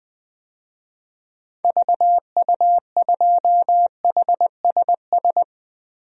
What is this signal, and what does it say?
Morse code, heard as a keyed tone, about 20 wpm, 700 Hz. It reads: VU2HSS
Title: DXCC-100 Call Signs - 6 Characters - 20wpm